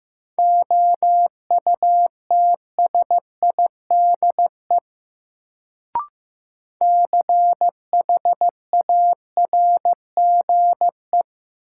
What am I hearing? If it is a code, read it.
Morse code, heard as a keyed tone, about 15 wpm, 700 Hz. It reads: OUTSIDE CHARGE
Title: Sentences from Top 900 Words - New Word Drill - 15wpm